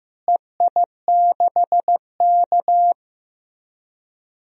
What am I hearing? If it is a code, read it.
Morse code, heard as a keyed tone, about 15 wpm, 700 Hz. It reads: EI6K
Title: DXCC-100 Call Signs - 4 Characters - Repeated 2x - 15wpm